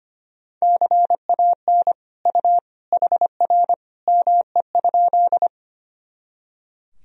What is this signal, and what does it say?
Morse code, heard as a keyed tone, about 25 wpm, 700 Hz. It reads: CAN U HR ME?
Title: Sentences from Top 300 Words and Abbreviations 25wpm